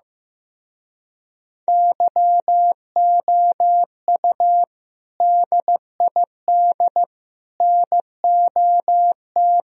Morse code, heard as a keyed tone, about 16 wpm, 700 Hz. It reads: YOU DID NOT
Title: Speed-Racing - Sets of 3 Words 15wpm